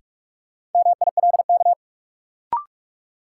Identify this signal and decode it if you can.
Morse code, heard as a keyed tone, about 46 wpm, 700 Hz. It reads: MILK
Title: Sentences from Top 1500 Words - New Word Drill - 45wpm